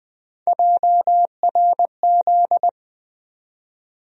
Morse code, heard as a keyed tone, about 20 wpm, 700 Hz. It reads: JRZ
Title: USA Call Sign Postfixes - 3 Letters 20wpm